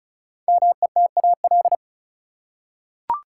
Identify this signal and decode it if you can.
Morse code, heard as a keyed tone, about 34 wpm, 700 Hz. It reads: METAL
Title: Five Letter Words - 35wpm